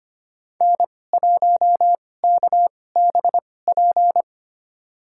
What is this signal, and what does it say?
Morse code, heard as a keyed tone, about 25 wpm, spaced as for 18 wpm, 700 Hz. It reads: N1KBP
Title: USA Call Signs - 5 Characters - 2x Character Spacing - 25wpm